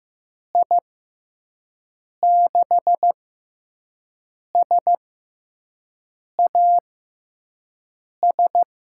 Morse code, heard as a keyed tone, about 15 wpm, spaced as for 5 wpm, 700 Hz. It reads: I6SAS
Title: USA Call Signs - 6 Characters - 6x Character Spacing - 15wpm